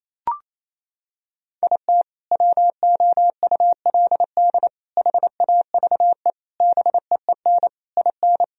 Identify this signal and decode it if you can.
Morse code, heard as a keyed tone, about 28 wpm, 700 Hz. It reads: IT WOULD HAVE BEEN IN
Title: Sentences from Top 1000 Words - Review Part 1 - 28wpm